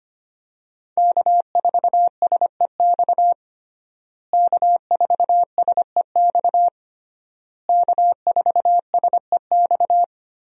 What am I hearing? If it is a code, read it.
Morse code, heard as a keyed tone, about 25 wpm, 700 Hz. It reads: K4SEX K4SEX K4SEX
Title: USA Call Signs - 5 Characters - Repeated 3x - 25wpm